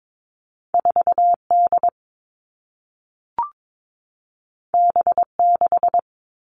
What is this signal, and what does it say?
Morse code, heard as a keyed tone, about 22 wpm, 700 Hz. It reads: 4D B6
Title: All Cut Numbers - 2-Digit 22wpm